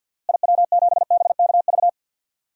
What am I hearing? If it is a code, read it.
Morse code, heard as a keyed tone, about 52 wpm, 700 Hz. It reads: IW7BKV
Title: DXCC-100 Call Signs - 6 Characters - Repeated 3x - 50wpm